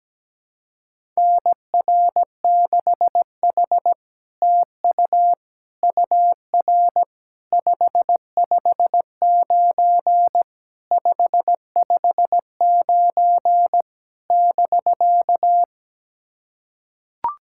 Morse code, heard as a keyed tone, about 17 wpm, 700 Hz. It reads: NR6H TU UR 559 559 <BK>
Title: SOTA QSO Practice 17wpm